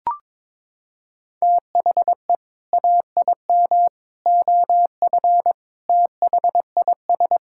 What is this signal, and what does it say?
Morse code, heard as a keyed tone, about 22 wpm, 700 Hz. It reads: THE AIM OF THIS
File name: practice / 22wpm / Sets of 7 Words - 22wpm